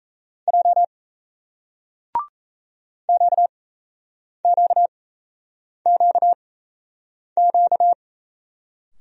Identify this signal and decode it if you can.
Morse code, heard as a keyed tone, about 38 wpm, spaced as for 8 wpm, 700 Hz. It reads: J QQQQ